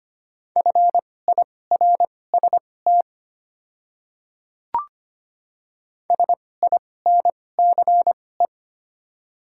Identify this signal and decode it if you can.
Morse code, heard as a keyed tone, about 25 wpm, spaced as for 18 wpm, 700 Hz. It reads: FIRST SINCE
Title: 2x Character Spacing - Top 300 Words - 25wpm